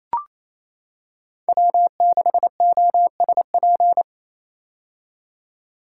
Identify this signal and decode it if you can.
Morse code, heard as a keyed tone, about 27 wpm, 700 Hz. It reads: W6OSP
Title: K1USN SST QSO Practice - 28wpm